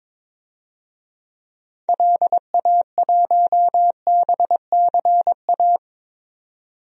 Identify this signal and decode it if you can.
Morse code, heard as a keyed tone, about 22 wpm, 700 Hz. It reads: LA1BCA
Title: DXCC-100 Call Signs - 6 Characters - 22wpm